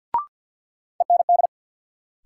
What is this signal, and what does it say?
Morse code, heard as a keyed tone, about 52 wpm, 700 Hz. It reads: END
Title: Sentences from Top 200 Words - New Word Drill - 50wpm